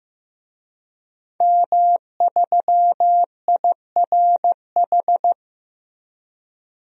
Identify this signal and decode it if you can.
Morse code, heard as a keyed tone, about 15 wpm, 700 Hz. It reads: M3IRH
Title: DXCC-100 Call Signs - 5 Characters - Repeated 2x - 15wpm